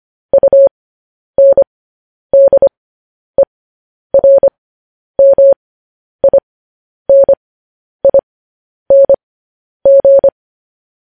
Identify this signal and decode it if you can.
Morse code, heard as a keyed tone, about 25 wpm, spaced as for 10 wpm, 550 Hz. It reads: UNDERMINING
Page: news headlines in morse code at 10 wpm